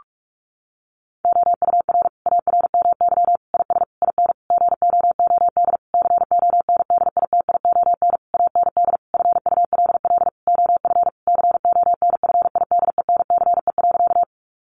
Morse code, heard as a keyed tone, about 44 wpm, 700 Hz. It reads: OUR ARMY IS IN GOOD CONDITION AND FULL OF CONFIDENCE.
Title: Sentences from Top 1500 Words 45wpm